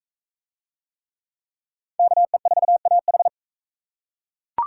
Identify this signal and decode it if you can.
Morse code, heard as a keyed tone, about 37 wpm, 700 Hz. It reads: KE4AH